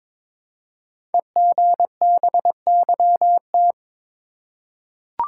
Morse code, heard as a keyed tone, about 22 wpm, 700 Hz. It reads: EGBYT
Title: Code Groups - 5 Letter:Numbers - 22wpm